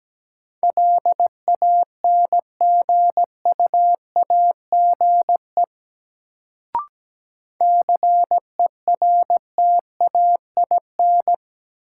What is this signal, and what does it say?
Morse code, heard as a keyed tone, about 17 wpm, 700 Hz. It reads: LANGUAGE CERTAIN
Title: Sentences from Top 500 Words - New Word Drill - 17wpm